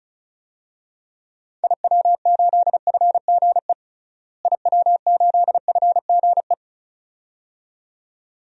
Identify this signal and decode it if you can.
Morse code, heard as a keyed tone, about 36 wpm, 700 Hz. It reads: IW8FGE IW8FGE
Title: DXCC-100 Call Signs - 6 Characters - Repeated 2x - 35wpm